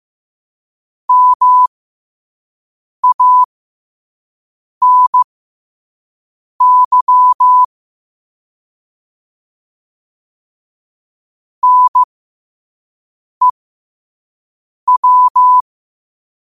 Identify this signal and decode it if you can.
Morse code, heard as a keyed tone, about 15 wpm, spaced as for 5 wpm, 1000 Hz. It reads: MANY NEW
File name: qotdfiles / 5wpm